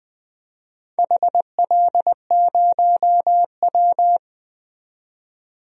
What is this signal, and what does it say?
Morse code, heard as a keyed tone, about 20 wpm, 700 Hz. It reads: HL0W
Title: DXCC-100 Call Signs - 4 Characters - 20wpm